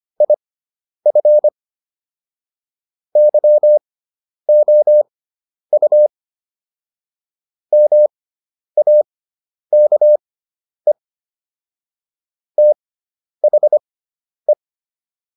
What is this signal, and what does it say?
Morse code, heard as a keyed tone, about 25 wpm, spaced as for 10 wpm, 600 Hz. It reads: IF YOU MAKE THE